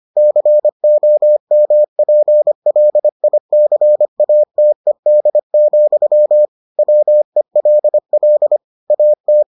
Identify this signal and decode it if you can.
Morse code, heard as a keyed tone, about 25 wpm, 600 Hz. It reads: COMPLICATED, WELL AT